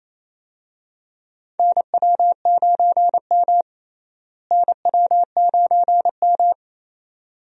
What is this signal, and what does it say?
Morse code, heard as a keyed tone, about 28 wpm, 700 Hz. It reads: NW9M NW9M
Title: USA Call Signs - 4 Characters - Repeated 2x - 28wpm